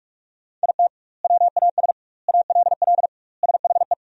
Morse code, heard as a keyed tone, about 46 wpm, 700 Hz. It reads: IT WAS ALL SHE
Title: 2x Word-Spacing - Sentences from Top 400 Words - 45wpm